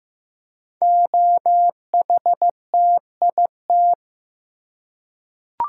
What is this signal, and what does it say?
Morse code, heard as a keyed tone, about 15 wpm, 700 Hz. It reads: OHTIT
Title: Code Groups - 5 Letters - 15wpm